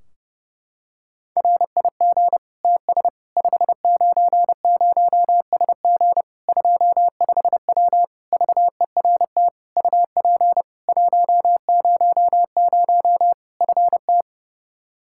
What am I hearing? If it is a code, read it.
Morse code, heard as a keyed tone, about 30 wpm, 700 Hz. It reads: RIG TS 590SG 25W VERT UP 100 FT
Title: Practice QSO - Exchange 2 - Info - 30wpm